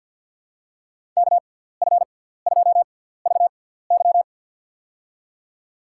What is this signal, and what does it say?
Morse code, heard as a keyed tone, about 52 wpm, spaced as for 17 wpm, 700 Hz. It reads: KF2VY
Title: USA Call Signs - 5 Characters - 6x Character Spacing - 50wpm